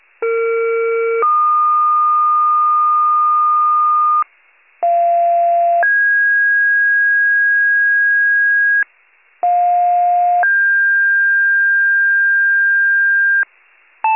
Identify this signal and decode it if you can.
Two-tone page, heard as a one-second tone, then a three-second tone, about 460 then 1220 Hz, about 690 then 1650 Hz, about 710 then 1600 Hz, with noise. It is heard from 0.2 to 13.4 s.